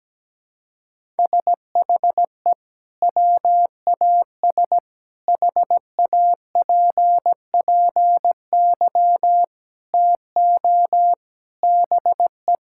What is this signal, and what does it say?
Morse code, heard as a keyed tone, about 17 wpm, 700 Hz. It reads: SHE WAS HAPPY TO BE
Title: Sentences from Top 1000 Words - Review Part 5 - 17wpm